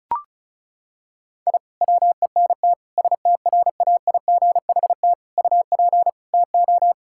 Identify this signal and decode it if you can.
Morse code, heard as a keyed tone, about 34 wpm, 700 Hz. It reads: I WENT STRAIGHT UP TO
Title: Sentences from Top 1000 Words - Review Part 6 - 35wpm